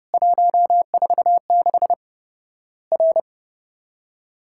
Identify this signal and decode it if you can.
Morse code, heard as a keyed tone, about 30 wpm, 700 Hz. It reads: 146 R
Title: Worked All Europe DX Contest - QTC Exchange - 30wpm